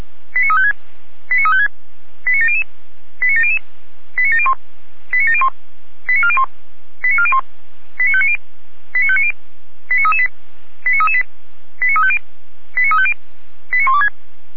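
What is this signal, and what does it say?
Five-tone selective call (ZVEI), heard as quick runs of five tones, tones of about 70 ms. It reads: 89356 89356 89700 89700 89701 89701 89401 89401 89590 89590 89338 89338 89350 89350 89126
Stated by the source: ZVEI five-tone selective call, address 89104, received by SDR at feuerwehr-orlamuende (fire brigade Orlamünde).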